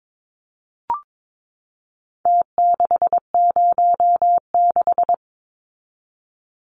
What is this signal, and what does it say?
Morse code, heard as a keyed tone, about 22 wpm, 700 Hz. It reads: T606
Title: Numbers - Contest Style - 4-Digit 22wpm